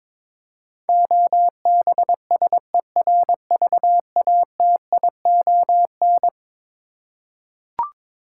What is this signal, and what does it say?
Morse code, heard as a keyed tone, about 21 wpm, 700 Hz. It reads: OBSERVATION
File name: practice / 22wpm / Eleven Letter Words - 22wpm